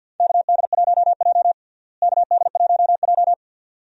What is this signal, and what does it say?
Morse code, heard as a keyed tone, about 53 wpm, 700 Hz. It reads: KD1J KD1J
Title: USA Call Signs - 4 Characters - Repeated 2x - 50wpm